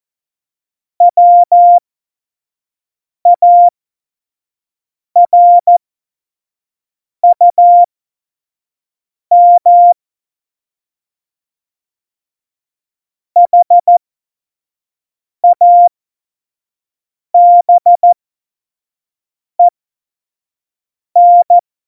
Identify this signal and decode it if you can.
Morse code, heard as a keyed tone, about 14 wpm, spaced as for 5 wpm, 700 Hz. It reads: WARUM HABEN
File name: Morsekurs / Witze / Witz 30. Oktober 2018, 25 F70